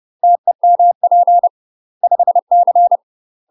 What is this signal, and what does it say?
Morse code, heard as a keyed tone, about 30 wpm, 700 Hz. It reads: TEMP 5C
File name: UK MASTER QSOs TRAINING / 30WPM / UK MASTER QSOs 0003